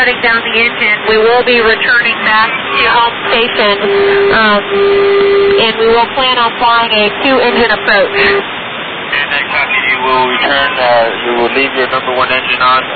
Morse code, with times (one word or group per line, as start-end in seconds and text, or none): none